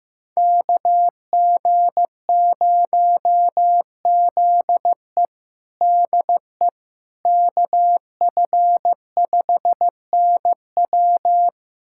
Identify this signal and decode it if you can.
Morse code, heard as a keyed tone, about 15 wpm, 700 Hz. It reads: KG0ZE DE KF5NW
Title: USA Callsign DE Callsign - 5 Characters - 15wpm